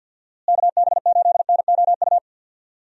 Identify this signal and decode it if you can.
Morse code, heard as a keyed tone, about 50 wpm, 700 Hz. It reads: KB8NOU
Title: USA Call Signs - 6 Characters - Repeated 2x - 50wpm